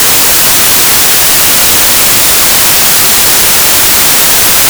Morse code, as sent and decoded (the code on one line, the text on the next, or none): none